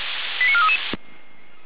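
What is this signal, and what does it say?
Five-tone selective call (ZVEI), heard as quick runs of five tones, tones of about 70 ms. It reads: 89433
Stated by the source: ZVEI five-tone call 88074 received by SDR.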